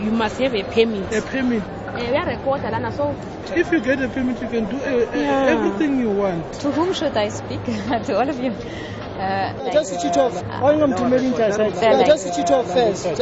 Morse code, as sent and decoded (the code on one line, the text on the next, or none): none